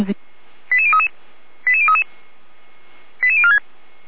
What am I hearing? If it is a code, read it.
Five-tone selective call (ZVEI), heard as quick runs of five tones, tones of about 80 ms, with speech. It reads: 88033 88033 88046